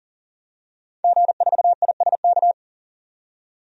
Morse code, heard as a keyed tone, about 40 wpm, 700 Hz. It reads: G4ISK
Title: DXCC-100 Call Signs - 5 Characters - 40wpm